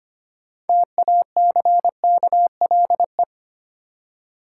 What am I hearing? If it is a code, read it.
Morse code, heard as a keyed tone, about 25 wpm, 700 Hz. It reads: TACKLE
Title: Six Letter Words - 25wpm